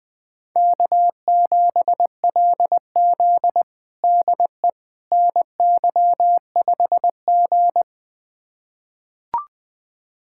Morse code, heard as a keyed tone, about 20 wpm, 700 Hz. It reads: K7LZ DE NY5G
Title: USA Callsign DE Callsign - 4 Characters - 20wpm